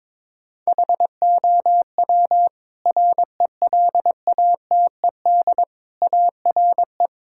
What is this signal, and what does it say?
Morse code, heard as a keyed tone, about 22 wpm, 700 Hz. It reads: HOW RELATED ARE
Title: Sentences from Top 1000 Words - Review Part 5 - 22wpm